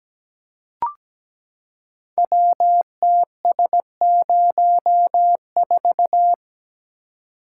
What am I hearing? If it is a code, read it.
Morse code, heard as a keyed tone, about 17 wpm, 700 Hz. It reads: WTS04
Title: Code Groups - 5 Letter:Numbers - 17wpm